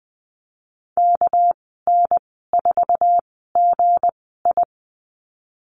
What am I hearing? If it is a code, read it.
Morse code, heard as a keyed tone, about 20 wpm, spaced as for 15 wpm, 700 Hz. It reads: KN4GI
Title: 200 Lesson 32g - Letter k - Call Signs 20wpm